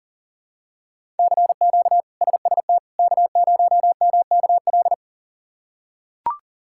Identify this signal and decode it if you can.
Morse code, heard as a keyed tone, about 37 wpm, 700 Hz. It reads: CQ SST K0MKL
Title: K1USN SST QSO Practice - 40wpm